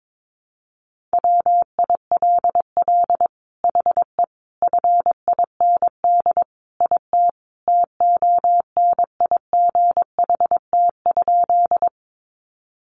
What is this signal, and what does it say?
Morse code, heard as a keyed tone, about 22 wpm, 700 Hz. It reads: WILL HE FIND IT TONIGHT?
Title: Sentences from Top 600 Words 22wpm